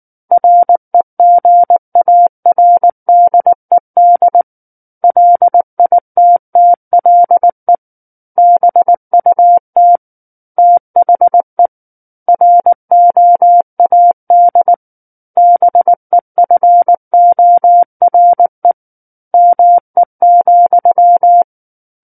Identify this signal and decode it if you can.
Morse code, heard as a keyed tone, about 19 wpm, 700 Hz. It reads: REGARDED LITTLE BUT THE ROAD BEFORE ME,